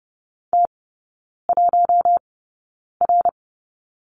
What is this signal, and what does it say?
Morse code, heard as a keyed tone, about 30 wpm, spaced as for 9 wpm, 700 Hz. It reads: T1R
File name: Lesson 17e - Letter u - Three Characters 30wpm